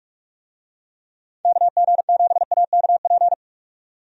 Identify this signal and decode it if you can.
Morse code, heard as a keyed tone, about 46 wpm, 700 Hz. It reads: KG7AKP
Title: USA Call Signs - 6 Characters - 45wpm